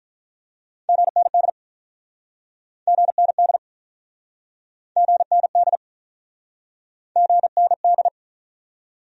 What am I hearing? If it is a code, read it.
Morse code, heard as a keyed tone, about 43 wpm, 700 Hz. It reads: GND GND GND GND
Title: Speed-Racing Abbreviations - 35wpm